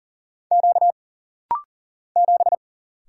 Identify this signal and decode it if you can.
Morse code, heard as a keyed tone, about 41 wpm, 700 Hz. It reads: Q 7
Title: Single Letter-Number - Rapid-Fire 40wpm